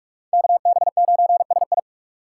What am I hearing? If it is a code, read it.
Morse code, heard as a keyed tone, about 44 wpm, 700 Hz. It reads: KB9SI